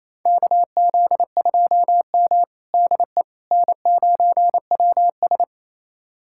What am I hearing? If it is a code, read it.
Morse code, heard as a keyed tone, about 28 wpm, 700 Hz. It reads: KZ2M DE N9WS
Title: USA Callsign DE Callsign - 4 Characters - 28wpm